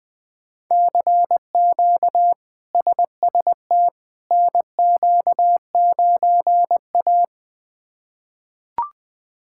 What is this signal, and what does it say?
Morse code, heard as a keyed tone, about 20 wpm, 700 Hz. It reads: CQ SST NQ9A
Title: K1USN SST QSO Practice - 20wpm